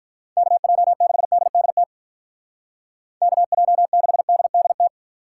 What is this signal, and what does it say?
Morse code, heard as a keyed tone, about 44 wpm, 700 Hz. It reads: KJ6DDT KJ6DDT
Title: Speed Racing - USA 2x3 Call Signs 35wpm